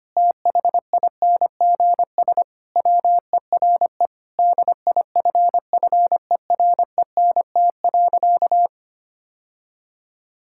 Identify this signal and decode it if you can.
Morse code, heard as a keyed tone, about 25 wpm, 700 Hz. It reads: THINGS WERE DIFFERENT.
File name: practice / 25wpm / Sentences with 3-Word Phrases - 25wpm